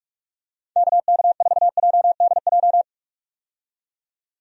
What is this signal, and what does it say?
Morse code, heard as a keyed tone, about 46 wpm, 700 Hz. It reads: KK4JDJ
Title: USA Call Signs - 6 Characters - 45wpm